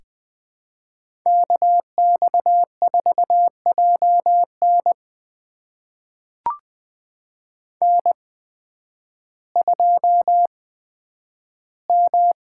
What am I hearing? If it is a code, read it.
Morse code, heard as a keyed tone, about 20 wpm, spaced as for 11 wpm, 700 Hz. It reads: KX4JN N2M
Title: USA Call Signs - 5 Characters - 8x Character Spacing - 20wpm